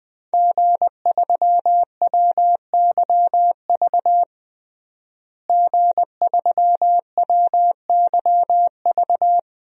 Morse code, heard as a keyed tone, about 20 wpm, 700 Hz. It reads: G3WYV G3WYV
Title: DXCC-100 Call Signs - 5 Characters - Repeated 3x - 20wpm